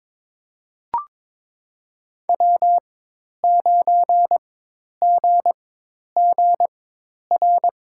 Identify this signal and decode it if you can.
Morse code, heard as a keyed tone, about 22 wpm, spaced as for 10 wpm, 700 Hz. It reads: W9GGR
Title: USA Call Signs - 5 Characters - 4x Character Spacing - 22wpm